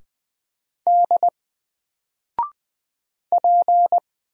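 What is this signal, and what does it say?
Morse code, heard as a keyed tone, about 20 wpm, 700 Hz. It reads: D P
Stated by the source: Single letter and number Morse drill at 20 wpm, from N4LCD.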